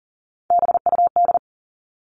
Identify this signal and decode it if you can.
Morse code, heard as a keyed tone, about 40 wpm, 700 Hz. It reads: BUD
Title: All Cut Numbers - 3-Digit 40wpm